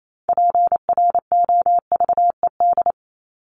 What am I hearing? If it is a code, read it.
Morse code, heard as a keyed tone, about 28 wpm, 700 Hz. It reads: PROVED
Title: ED Suffix - Top 100 Words with Base Words 28wpm